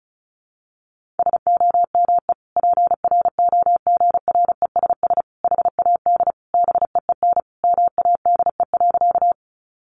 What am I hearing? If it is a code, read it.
Morse code, heard as a keyed tone, about 34 wpm, 700 Hz. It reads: SOME PROGRESS HAD BEEN MADE.
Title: Sentences from Top 1400 Words 35wpm